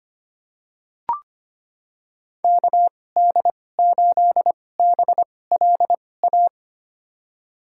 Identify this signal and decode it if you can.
Morse code, heard as a keyed tone, about 25 wpm, spaced as for 18 wpm, 700 Hz. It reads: KD8BLA
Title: USA Call Signs - 2x Character Spacing - 25wpm